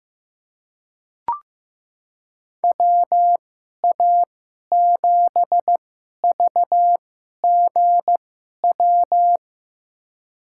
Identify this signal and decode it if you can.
Morse code, heard as a keyed tone, about 15 wpm, spaced as for 11 wpm, 700 Hz. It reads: WA7VGW
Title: USA Call Signs - 6 Characters - 2x Character Spacing - 15wpm